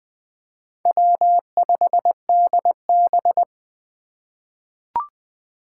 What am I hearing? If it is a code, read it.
Morse code, heard as a keyed tone, about 20 wpm, 700 Hz. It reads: W5DB
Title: USA Call Signs - 4 Characters - 2x Character Spacing - 20wpm